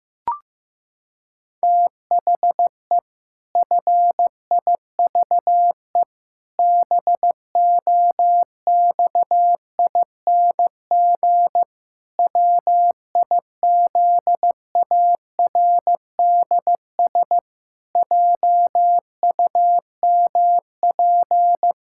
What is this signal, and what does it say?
Morse code, heard as a keyed tone, about 15 wpm, 700 Hz. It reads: THE FIVE BOXING WIZARDS JUMP
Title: Pangram Challenge v2 - 15wpm